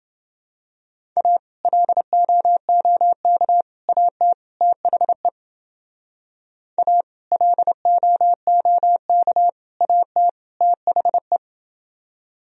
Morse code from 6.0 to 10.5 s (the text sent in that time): A LOOK AT